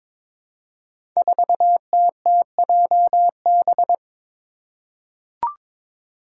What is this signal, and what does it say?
Morse code, heard as a keyed tone, about 22 wpm, 700 Hz. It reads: 4TTJB
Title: Five Challenging Letter-Number Combinations 22wpm